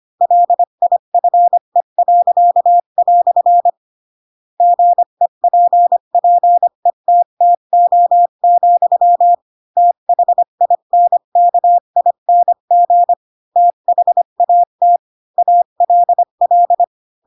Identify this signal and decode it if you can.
Morse code, heard as a keyed tone, about 25 wpm, 700 Hz. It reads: LIFE." GEPPETTO, THINKING THAT ALL
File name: Pinocchio / Pinocchio ch7 20 25 ik1hge